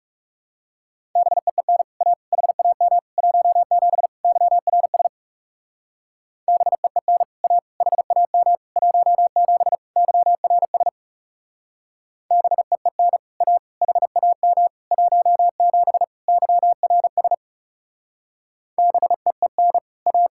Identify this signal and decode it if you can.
Morse code, heard as a keyed tone, about 37 wpm, 700 Hz. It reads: BEEN A HAM 17 YRS BEEN A HAM 17 YRS BEEN A HAM 17 YRS BEEN A
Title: Speed-Racing Common QSO Phrases Encore - 30wpm